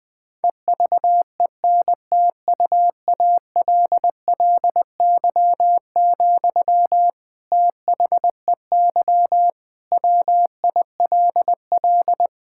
Morse code, heard as a keyed tone, about 19 wpm, 700 Hz. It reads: EVENTUALLY, THEY WILL
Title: Sentences from Top 1000 Words - Review Part 6 - 20wpm